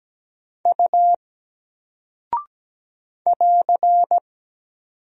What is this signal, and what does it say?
Morse code, heard as a keyed tone, about 17 wpm, 700 Hz. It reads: U <AR>
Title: Single Character (Letter:Number:Punctuation:Prosign) 17wpm